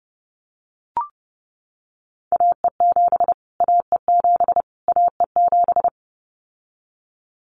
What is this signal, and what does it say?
Morse code, heard as a keyed tone, about 30 wpm, 700 Hz. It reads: AE7 AE7 AE7
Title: Callsign Prefixes for Top 25 DXCC - Repeated 3x - 30wpm